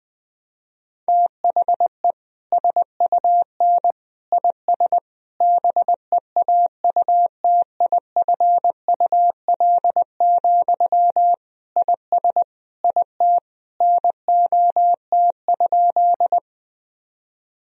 Morse code, heard as a keyed tone, about 20 wpm, 700 Hz. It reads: THE SUN IS BEAUTIFUL, IS IT NOT?
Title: Sentences from Top 1000 Words - Review Part 1 - 20wpm